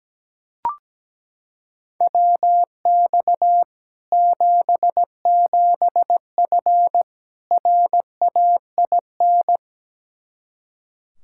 Morse code, heard as a keyed tone, about 17 wpm, 700 Hz. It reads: WX 77F RAIN
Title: Practice QSO - Exchange 2 - Info - 17wpm